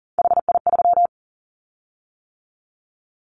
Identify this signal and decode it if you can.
Morse code, heard as a keyed tone, about 40 wpm, 700 Hz. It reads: HI3
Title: Callsign Prefixes for Top 100 DXCC - 40wpm